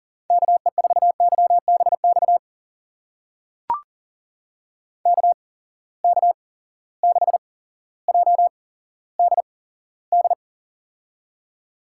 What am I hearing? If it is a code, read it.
Morse code, heard as a keyed tone, about 39 wpm, spaced as for 17 wpm, 700 Hz. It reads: KE4YBX KK6JDD
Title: USA Call Signs - 6 Characters - 8x Character Spacing - 40wpm